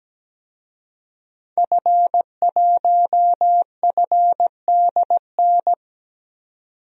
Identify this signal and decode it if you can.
Morse code, heard as a keyed tone, about 17 wpm, 700 Hz. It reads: F1FDN
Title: DXCC-100 Call Signs - 5 Characters - 17wpm